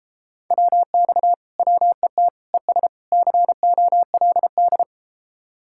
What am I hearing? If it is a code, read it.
Morse code, heard as a keyed tone, about 32 wpm, 700 Hz. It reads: WX WET ES COLD